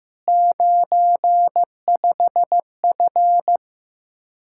Morse code, heard as a keyed tone, about 15 wpm, 700 Hz. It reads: 95F
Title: Practice QSO - Exchange 2 - Info - 15wpm